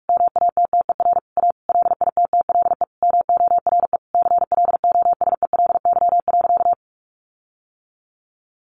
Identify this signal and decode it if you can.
Morse code, heard as a keyed tone, about 44 wpm, 700 Hz. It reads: MATTER A LITTLE MORE CLOSELY.
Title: Sentences from Top 1400 Words 45wpm